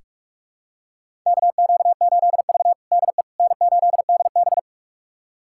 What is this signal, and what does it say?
Morse code, heard as a keyed tone, about 46 wpm, 700 Hz. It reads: KQ8V DE N8DB
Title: USA Callsign DE Callsign - 4 Characters - 45wpm